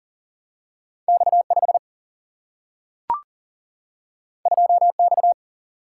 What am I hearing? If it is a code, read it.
Morse code, heard as a keyed tone, about 42 wpm, 700 Hz. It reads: X 5 2 X
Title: Single-Letter Single-Number 40wpm